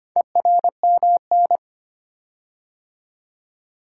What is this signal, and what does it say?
Morse code, heard as a keyed tone, about 25 wpm, 700 Hz. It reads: ERMN
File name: practice / 25wpm / Four Letter-Number Combinations - 25wpm